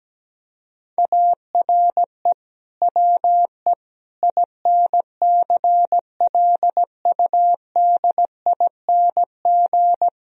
Morse code, heard as a keyed tone, about 17 wpm, 700 Hz. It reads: ARE WE INCLUDING
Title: Sentences from Top 1000 Words - Review Part 4 - 17wpm